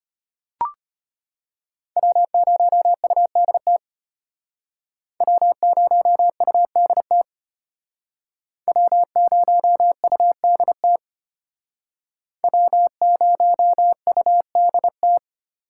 Morse code, heard as a keyed tone, about 33 wpm, 700 Hz. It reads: W0UDT W0UDT W0UDT W0UDT